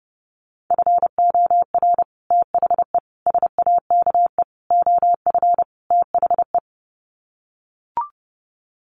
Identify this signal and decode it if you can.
Morse code, heard as a keyed tone, about 30 wpm, 700 Hz. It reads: FOR THE SAKE OF THE
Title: Sets of 5 Words Encore 30wpm